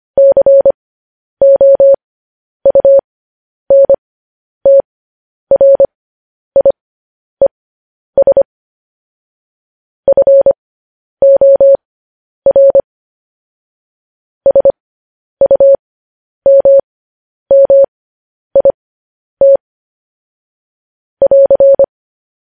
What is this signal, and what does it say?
Morse code, heard as a keyed tone, about 25 wpm, spaced as for 10 wpm, 550 Hz. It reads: COUNTRIES FOR SUMMIT <AR>
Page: news-headlines-in-morse-code-at-10-wpm